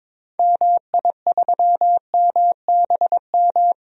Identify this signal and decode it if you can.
Morse code, heard as a keyed tone, about 22 wpm, 700 Hz. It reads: MI3MBM
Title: DXCC-100 Call Signs - 6 Characters - 22wpm